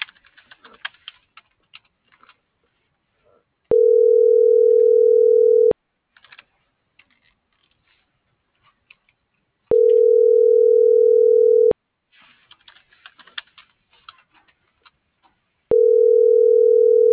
Ringback tone (North America), 440 + 480 Hz.